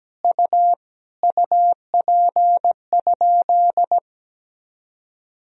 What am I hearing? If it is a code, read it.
Morse code, heard as a keyed tone, about 17 wpm, 700 Hz. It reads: U UP?
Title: Sentences from Top 500 Words and Abbreviations 17wpm